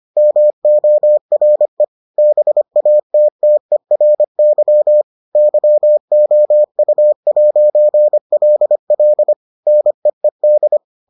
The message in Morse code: MORE BATTERY YOU'LL NEED

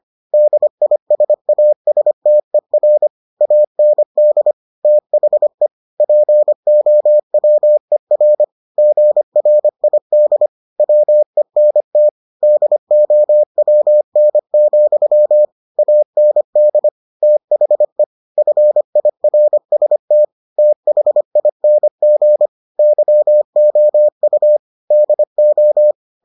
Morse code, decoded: DISASTER AND THE POWER GRID WENT DOWN, AND THE FIRST THING YOU DO

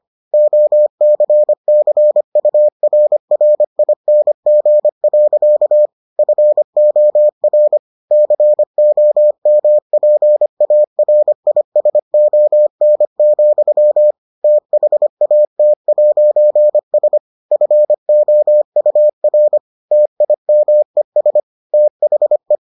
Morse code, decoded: OCCURRING. FOR COMPARISON, THAT'S FOUR TIMES THE

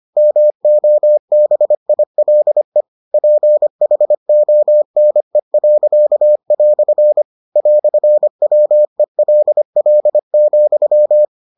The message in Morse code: MOBILE PHONE." "WELL,

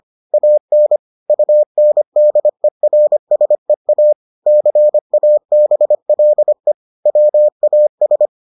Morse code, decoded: AN UNDERSEA CABLE WAS

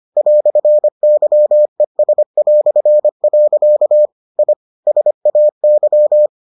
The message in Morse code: "YES". I SAY

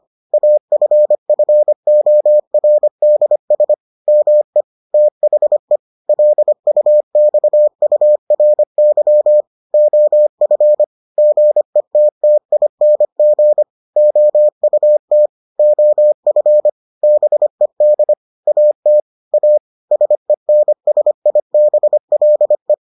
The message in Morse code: AFFORDS ME THE LUXURY OF GETTING OUT OF BED AT A SENSIBLE

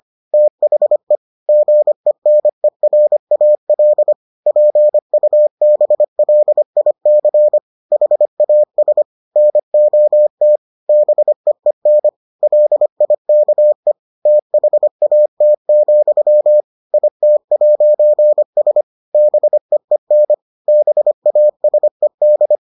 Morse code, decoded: THE GENERAL PUBLIC HAS NOT BEEN LIKE THAT, IT'S BEEN BASED

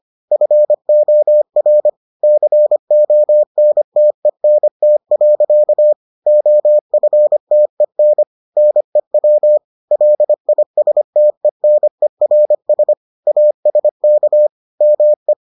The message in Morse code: FOR CONTENT. OFTEN NEW LISTENERS ASK ME

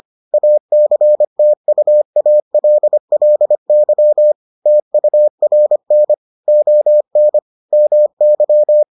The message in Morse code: ACTUALLY TURN ON MY

